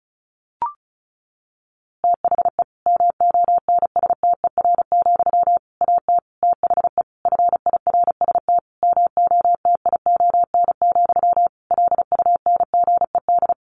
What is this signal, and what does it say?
Morse code, heard as a keyed tone, about 34 wpm, 700 Hz. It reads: THE MONSTER, AT THE FIRST MOTION, LUNGED
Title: A Sound Of Thunder 35wpm